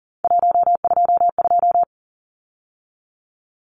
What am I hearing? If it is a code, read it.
Morse code, heard as a keyed tone, about 40 wpm, 700 Hz. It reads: 122
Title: Numbers - 3-Digit 40wpm